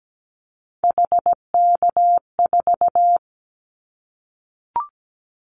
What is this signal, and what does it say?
Morse code, heard as a keyed tone, about 17 wpm, 700 Hz. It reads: HK4
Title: Callsign Prefixes for Top 100 DXCC - 17wpm